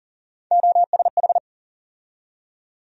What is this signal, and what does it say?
Morse code, heard as a keyed tone, about 40 wpm, 700 Hz. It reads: OSH
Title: USA Call Sign Postfixes - 3 Letters 40wpm